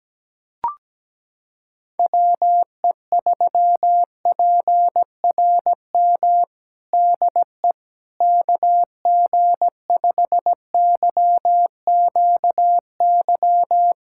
Morse code, heard as a keyed tone, about 17 wpm, 700 Hz. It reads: WE3PRM DE KG5YQY
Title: USA Callsign DE Callsign - 6 Characters - 17wpm